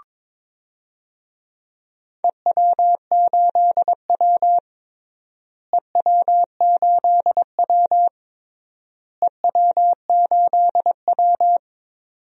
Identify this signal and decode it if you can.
Morse code, heard as a keyed tone, about 22 wpm, 700 Hz. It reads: EW8W EW8W EW8W
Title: DXCC-100 Call Signs - 4 Characters - Repeated 3x - 22wpm